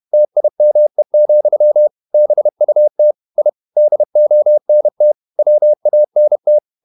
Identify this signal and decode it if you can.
Morse code, heard as a keyed tone, about 31 wpm, 600 Hz. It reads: TIME, BUT I DONT WANT
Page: Never 31wpm | CW med Gnister